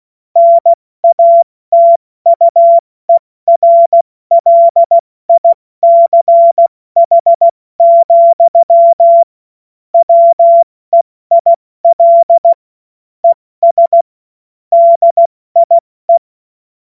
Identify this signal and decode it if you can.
Morse code, heard as a keyed tone, about 16 wpm, 680 Hz. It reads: NATUERLICH, WEIL ES DIE
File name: Morsekurs / Witze / Witz 03. Januar 2023 70 F80